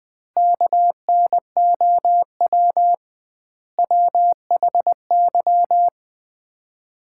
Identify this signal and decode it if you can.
Morse code, heard as a keyed tone, about 20 wpm, 700 Hz. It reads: KNOW WHY
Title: 2x Word-Spacing - Sets of 5 Words - 20wpm